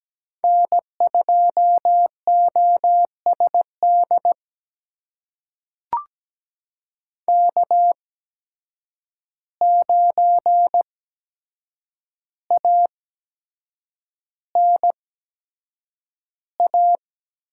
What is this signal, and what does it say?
Morse code, heard as a keyed tone, about 17 wpm, spaced as for 7 wpm, 700 Hz. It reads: N2OSD K9ANA
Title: USA Call Signs - 5 Characters - 8x Character Spacing - 17wpm